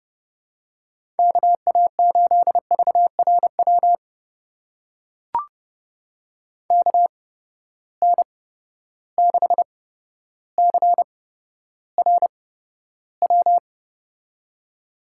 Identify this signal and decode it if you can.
Morse code, heard as a keyed tone, about 30 wpm, spaced as for 13 wpm, 700 Hz. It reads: KA8VRW KN6CRW
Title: USA Call Signs - 6 Characters - 8x Character Spacing - 30wpm